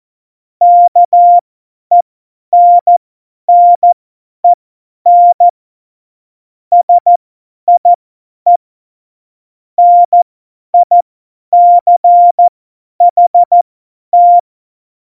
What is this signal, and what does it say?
Morse code, heard as a keyed tone, about 14 wpm, spaced as for 10 wpm, 700 Hz. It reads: KENNEN SIE NICHT
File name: Morsekurs / Witze / Witz 04. Februar 2020, 50 F70